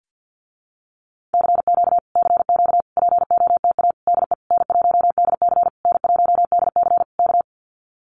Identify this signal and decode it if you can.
Morse code, heard as a keyed tone, about 50 wpm, 700 Hz. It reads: CQ CQ POTA DE N1DC N1DC K
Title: Common QSO Phrases Encore 50wpm